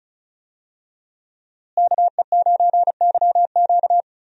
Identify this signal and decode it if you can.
Morse code, heard as a keyed tone, about 34 wpm, 700 Hz. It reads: KE9YQ
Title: USA Call Signs - 5 Characters - Repeated 2x - 35wpm